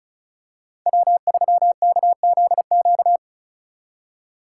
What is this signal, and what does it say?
Morse code, heard as a keyed tone, about 33 wpm, 700 Hz. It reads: W3KZQ
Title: Speed Racing - USA 1x3 Call Signs 35wpm